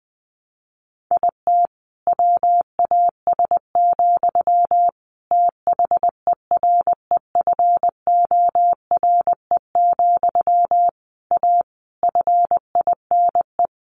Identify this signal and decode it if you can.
Morse code, heard as a keyed tone, about 20 wpm, 700 Hz. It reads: IT WAS, THEREFORE, A FINE
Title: Sentences from Top 1400 Words 20wpm